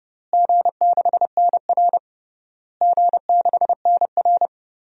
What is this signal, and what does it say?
Morse code, heard as a keyed tone, about 32 wpm, 700 Hz. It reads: G6NR G6NR
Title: DXCC-100 Call Signs - 4 Characters - Repeated 3x - 30wpm